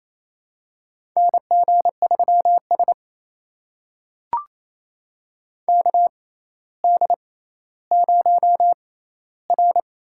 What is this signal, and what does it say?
Morse code, heard as a keyed tone, about 28 wpm, spaced as for 14 wpm, 700 Hz. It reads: NG3S KD0R
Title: USA Call Signs - 6x Character Spacing - 28wpm